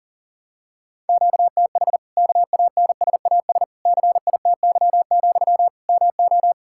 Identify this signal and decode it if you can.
Morse code, heard as a keyed tone, about 40 wpm, 700 Hz. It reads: QTH KANSAS CITY, MO
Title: Practice QSO - Exchange 1 - Beginning - 40wpm